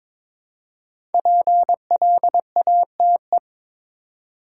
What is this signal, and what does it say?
Morse code, heard as a keyed tone, about 22 wpm, 700 Hz. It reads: PLATE
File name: 5 Letter Words - First 3 Letters Given - 22wpm